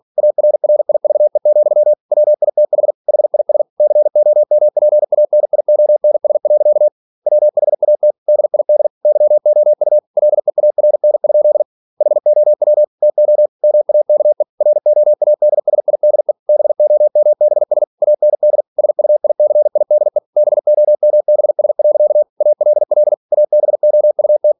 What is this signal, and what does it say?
Morse code, heard as a keyed tone, about 46 wpm, 600 Hz. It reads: ARRIVE, WITH HIS COMPANIONS. WHAT DID YOU LEARN? HOW TO MAKE ROADSIDE BOMBS AND SUICIDE BOMBS. ALL ABOUT
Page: Never 47wpm